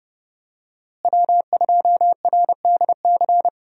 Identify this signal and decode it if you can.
Morse code, heard as a keyed tone, about 28 wpm, 700 Hz. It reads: W2RDC
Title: USA Call Signs - 5 Characters - Repeated 3x - 30wpm